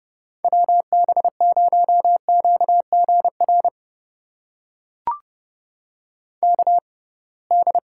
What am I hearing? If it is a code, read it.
Morse code, heard as a keyed tone, about 32 wpm, spaced as for 23 wpm, 700 Hz. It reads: WB0QGR KD
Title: USA Call Signs - 6x Character Spacing - 30wpm